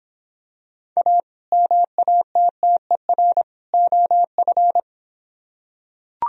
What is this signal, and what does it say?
Morse code, heard as a keyed tone, about 25 wpm, 700 Hz. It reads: A MATTER OF